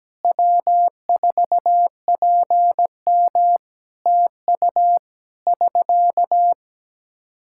Morse code, heard as a keyed tone, about 17 wpm, 700 Hz. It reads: W4PM TU <SK>
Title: Practice QSO - Exchange 4 - Ending - 17wpm